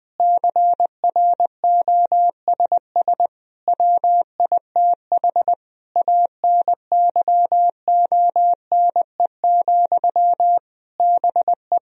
Morse code, heard as a keyed tone, about 20 wpm, 700 Hz. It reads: CROSS WITH ANYONE, BE
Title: Sentences from Top 1000 Words - Review Part 1 - 20wpm